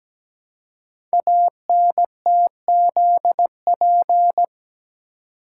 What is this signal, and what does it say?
Morse code, heard as a keyed tone, about 17 wpm, 700 Hz. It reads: ANTZP
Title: Code Groups - 5 Letters - 17wpm